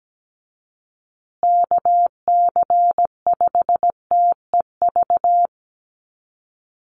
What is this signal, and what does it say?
Morse code, heard as a keyed tone, about 17 wpm, 700 Hz. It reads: KC5TEV
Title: USA Call Signs Repeated 2x 17wpm